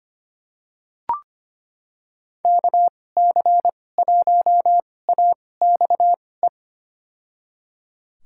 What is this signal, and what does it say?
Morse code, heard as a keyed tone, about 25 wpm, spaced as for 18 wpm, 700 Hz. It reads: KC1AXE
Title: USA Call Signs - 2x Character Spacing - 25wpm